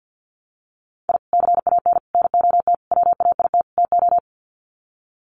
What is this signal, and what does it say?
Morse code, heard as a keyed tone, about 50 wpm, 700 Hz. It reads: I CAN NOT WAIT TO
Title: Sets of 5 Words Encore 50wpm